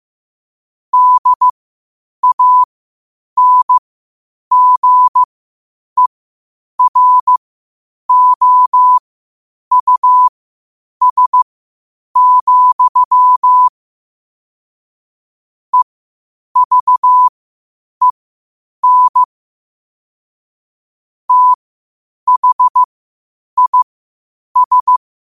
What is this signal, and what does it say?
Morse code, heard as a keyed tone, about 15 wpm, spaced as for 8 wpm, 1000 Hz. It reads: DANGEROUS, EVEN THIS
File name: qotdfiles / 8wpm